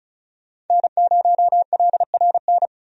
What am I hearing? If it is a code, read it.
Morse code, heard as a keyed tone, about 34 wpm, 700 Hz. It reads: N0LRN
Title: USA Call Signs - 5 Characters - 35wpm